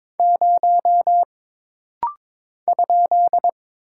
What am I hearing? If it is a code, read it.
Morse code, heard as a keyed tone, about 22 wpm, 700 Hz. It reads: 0 ?
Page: This rapid-fire timing is geared to push your ICR (Instant Character Recognition) skills.